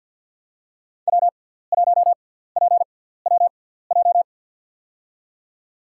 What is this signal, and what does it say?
Morse code, heard as a keyed tone, about 50 wpm, spaced as for 17 wpm, 700 Hz. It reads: W1PWJ
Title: USA Call Signs - 5 Characters - 6x Character Spacing - 50wpm